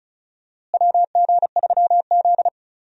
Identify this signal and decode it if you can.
Morse code, heard as a keyed tone, about 34 wpm, 700 Hz. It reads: WG3Z
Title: USA Call Signs - 4 Characters - Repeated 2x - 35wpm